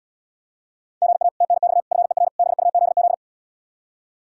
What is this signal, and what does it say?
Morse code, heard as a keyed tone, about 80 wpm, 700 Hz. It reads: GA TNX FER BUZZ